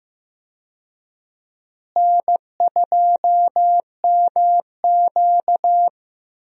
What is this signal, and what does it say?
Morse code, heard as a keyed tone, about 15 wpm, 700 Hz. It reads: N2MQ
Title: USA Call Signs - 4 Characters - Repeated 3x - 15wpm